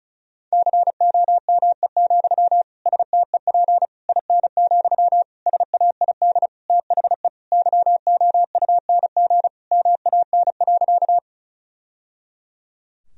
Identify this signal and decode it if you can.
Morse code, heard as a keyed tone, about 34 wpm, 700 Hz. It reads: COME, STEP IN, SAID THE YOUNG MAN.
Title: Sentences from Top 1000 Words - Review Part 2 - 35wpm